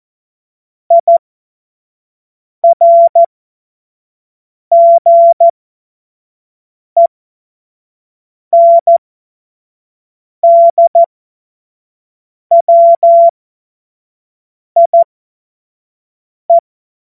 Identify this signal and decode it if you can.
Morse code, heard as a keyed tone, about 14 wpm, spaced as for 5 wpm, 680 Hz. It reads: IRGENDWIE
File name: Witz-08-Juli-2025-25-F70